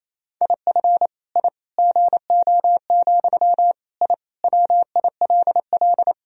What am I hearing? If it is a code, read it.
Morse code, heard as a keyed tone, about 28 wpm, 700 Hz. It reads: IF I GO, I WILL
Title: Sentences from Top 1000 Words - Review Part 3 - 28wpm